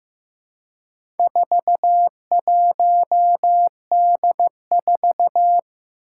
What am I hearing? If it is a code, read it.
Morse code, heard as a keyed tone, about 15 wpm, 700 Hz. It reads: 41D4
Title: Four Challenging Letter-Number Combinations 15wpm